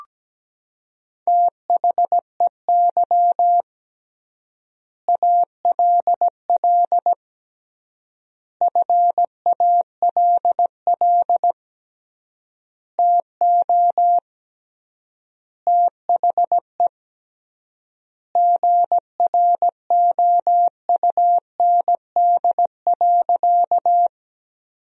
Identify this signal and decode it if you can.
Morse code, heard as a keyed tone, about 17 wpm, 700 Hz. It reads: THEY ALL FALL TO THE GROUND.
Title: 3x Word-Spacing - Sentences from Top 500 Words - 17wpm